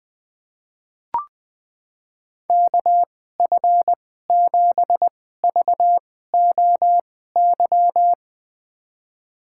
Morse code, heard as a keyed tone, about 20 wpm, spaced as for 14 wpm, 700 Hz. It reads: KF7VOY
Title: USA Call Signs - 6 Characters - 2x Character Spacing - 20wpm